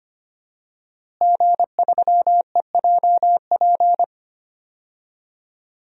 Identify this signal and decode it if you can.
Morse code, heard as a keyed tone, about 25 wpm, 700 Hz. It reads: G3EJP
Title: DXCC-100 Call Signs - 5 Characters - Repeated 3x - 25wpm